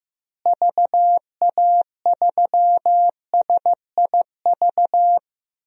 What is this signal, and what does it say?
Morse code, heard as a keyed tone, about 15 wpm, 700 Hz. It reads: VA3SIV
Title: DXCC-100 Call Signs - 6 Characters - Repeated 2x - 15wpm